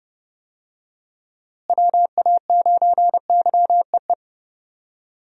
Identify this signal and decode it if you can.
Morse code, heard as a keyed tone, about 32 wpm, 700 Hz. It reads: WA9YEE
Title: USA Call Signs - 6 Characters - Repeated 3x - 30wpm